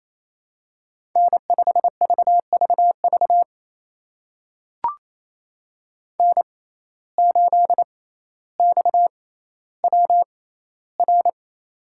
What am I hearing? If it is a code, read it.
Morse code, heard as a keyed tone, about 28 wpm, spaced as for 14 wpm, 700 Hz. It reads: N5VVV N8XWR
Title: USA Call Signs - 5 Characters - 6x Character Spacing - 28wpm